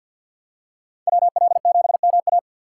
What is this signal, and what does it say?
Morse code, heard as a keyed tone, about 53 wpm, 700 Hz. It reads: WL7MA
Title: USA Call Signs - 5 Characters - 6x Character Spacing - 50wpm